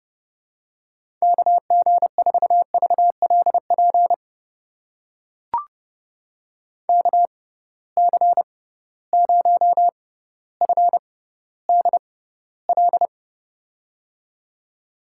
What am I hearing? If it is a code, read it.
Morse code, heard as a keyed tone, about 31 wpm, spaced as for 16 wpm, 700 Hz. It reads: KG4VLP KC0FDL
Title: USA Call Signs - 6 Characters - 6x Character Spacing - 30wpm